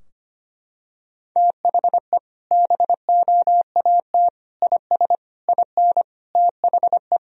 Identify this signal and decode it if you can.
Morse code, heard as a keyed tone, about 25 wpm, 700 Hz. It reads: THE BOAT IS IN THE
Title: Sentences from Top 1000 Words - Review Part 2 - 25wpm